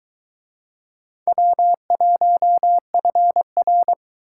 Morse code, heard as a keyed tone, about 23 wpm, 700 Hz. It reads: W1FR